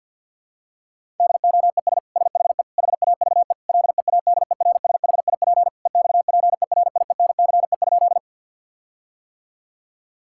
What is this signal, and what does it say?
Morse code, heard as a keyed tone, about 50 wpm, 700 Hz. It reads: DOES SHE HAVE LEADERSHIP EXPERIENCE?
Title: Sentences from Top 1600 Words 50wpm